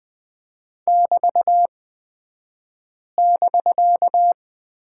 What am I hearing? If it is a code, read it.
Morse code, heard as a keyed tone, about 20 wpm, spaced as for 5 wpm, 700 Hz. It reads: <BT><BK>
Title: 249 Lesson 40d - BK - Two Characters 20wpm